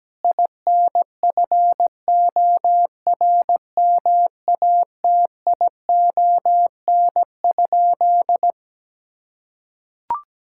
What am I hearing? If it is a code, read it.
Morse code, heard as a keyed tone, about 17 wpm, 700 Hz. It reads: INFORMATION?
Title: Sentences from Top 1000 Words - Review Part 6 - 17wpm